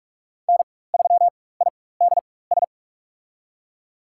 Morse code, heard as a keyed tone, about 43 wpm, spaced as for 21 wpm, 690 Hz. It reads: N3IDS
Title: USA Call Signs - 5 Characters - 4x Character Spacing - 45wpm